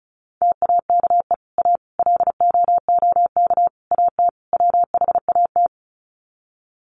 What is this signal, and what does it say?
Morse code, heard as a keyed tone, about 34 wpm, 700 Hz. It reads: TAKE A LOOK AT WHAT
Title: Sets of 5 Words Encore2 35wpm